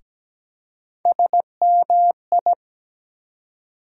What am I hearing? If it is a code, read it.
Morse code, heard as a keyed tone, about 17 wpm, 700 Hz. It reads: SMI
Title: USA Call Sign Postfixes - 3 Letters 17wpm